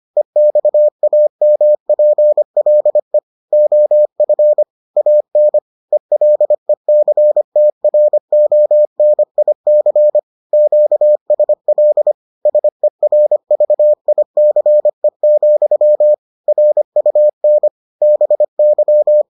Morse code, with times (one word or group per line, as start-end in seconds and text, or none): 0.2-3.2 EXAMPLE
3.5-4.6 OF
5.0-5.6 AN
5.9-10.2 ELECTRONIC
10.5-12.1 QSL
12.5-16.1 SERVICE,
16.5-17.7 RUN
18.0-19.2 BY